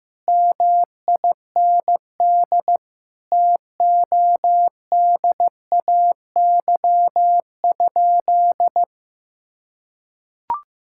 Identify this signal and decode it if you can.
Morse code, heard as a keyed tone, about 15 wpm, 700 Hz. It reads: MIND TODAY?
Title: Sentences from Top 1000 Words - Review Part 6 - 15wpm